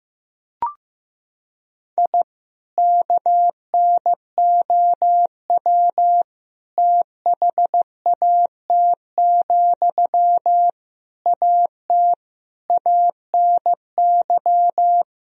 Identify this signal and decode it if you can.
Morse code, heard as a keyed tone, about 15 wpm, 700 Hz. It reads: I KNOW THAT, AT ANY
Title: Sentences from Top 1000 Words - Review Part 4 - 15wpm